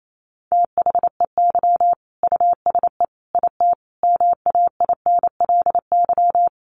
Morse code, heard as a keyed tone, about 28 wpm, 700 Hz. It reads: THEY USE IT MAINLY
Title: Sentences from Top 1400 Words 28wpm